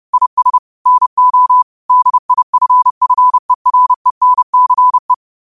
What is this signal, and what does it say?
Morse code, heard as a keyed tone, about 30 wpm, 1000 Hz. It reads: IS NO DIFFERENCE
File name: qotdfiles / 30wpm